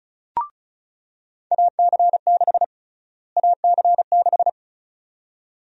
Morse code, heard as a keyed tone, about 36 wpm, 700 Hz. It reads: AC6 AC6
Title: USA Call Sign Prefixes - Repeated 2x - 35wpm